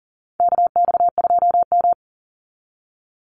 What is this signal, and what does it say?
Morse code, heard as a keyed tone, about 40 wpm, 700 Hz. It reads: KX2M
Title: USA Call Signs 40wpm